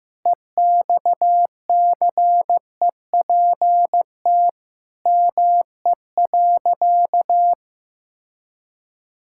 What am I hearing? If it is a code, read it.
Morse code, heard as a keyed tone, about 15 wpm, 700 Hz. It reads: EXCEPT ME.
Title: Sentences from Top 1600 Words 15wpm